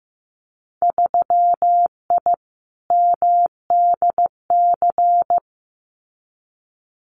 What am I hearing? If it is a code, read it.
Morse code, heard as a keyed tone, about 15 wpm, 700 Hz. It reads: 3I MDC
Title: Winter Field Day Exchanges Encore 15wpm